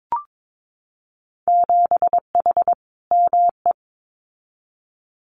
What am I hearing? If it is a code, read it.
Morse code, heard as a keyed tone, about 22 wpm, 700 Hz. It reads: 7H ME
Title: Winter Field Day Exchanges Encore 22wpm